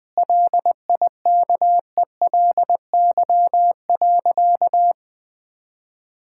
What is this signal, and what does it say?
Morse code, heard as a keyed tone, about 20 wpm, 700 Hz. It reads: LIKELY.
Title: Sentences from Top 1600 Words 20wpm